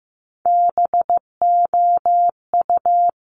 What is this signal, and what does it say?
Morse code, heard as a keyed tone, about 15 wpm, 700 Hz. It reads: BOU
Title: Top Three Letter Combinations in Words 15wpm